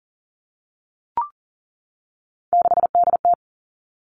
Morse code, heard as a keyed tone, about 42 wpm, 690 Hz. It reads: 6DT
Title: Three Challenging Letter-Number Combinations 40wpm